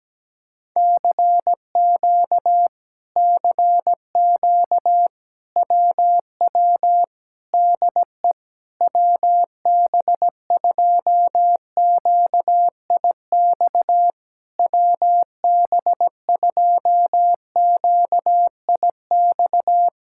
Morse code, heard as a keyed tone, about 17 wpm, 700 Hz. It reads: CQ CQ WW DE WB2QIX WB2QIX
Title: Speed-Racing Common QSO Phrases Encore - 17wpm